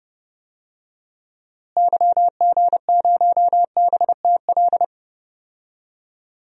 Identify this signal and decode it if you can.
Morse code, heard as a keyed tone, about 31 wpm, 700 Hz. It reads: YG0BTL
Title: DXCC-100 Call Signs - 6 Characters - Repeated 2x - 30wpm